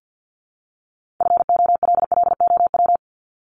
Morse code, heard as a keyed tone, about 50 wpm, 700 Hz. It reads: FOLLOW
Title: ING Suffix - Top 100 Words with Base Words 50wpm